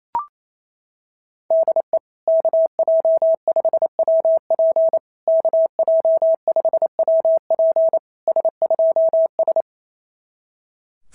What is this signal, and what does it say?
Morse code, heard as a keyed tone, about 27 wpm, 660 Hz. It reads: DE KJ5WP KJ5WP S2S
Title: SOTA QSO Practice 28wpm